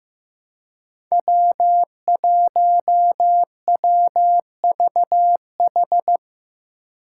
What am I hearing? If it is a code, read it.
Morse code, heard as a keyed tone, about 15 wpm, 700 Hz. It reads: W1WVH
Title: Five Challenging Letter-Number Combinations 15wpm